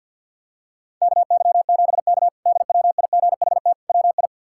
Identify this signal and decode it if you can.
Morse code, heard as a keyed tone, about 51 wpm, 700 Hz. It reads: KY7K DWIGHT WI
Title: K1USN SST QSO Practice - 50wpm